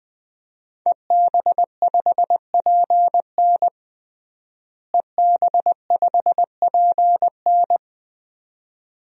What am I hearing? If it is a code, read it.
Morse code, heard as a keyed tone, about 21 wpm, 700 Hz. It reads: EB5PN EB5PN
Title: DXCC-100 Call Signs - 5 Characters - Repeated 2x - 20wpm